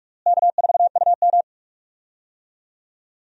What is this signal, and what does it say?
Morse code, heard as a keyed tone, about 46 wpm, 700 Hz. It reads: K4UM
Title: USA Call Signs - 4 Characters - Repeated 2x - 45wpm